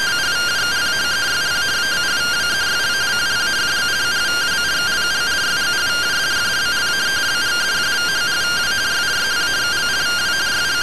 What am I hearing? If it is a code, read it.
Two-tone teleprinter signal, about 170 Hz shift.
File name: findings / MR CORONAL 55.gen-06 rtty rx